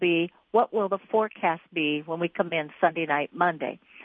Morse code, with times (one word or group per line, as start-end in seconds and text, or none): none